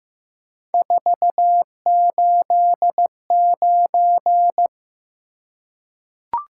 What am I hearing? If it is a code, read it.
Morse code, heard as a keyed tone, about 15 wpm, 700 Hz. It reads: 489
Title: Top 180 QSO Elements 15wpm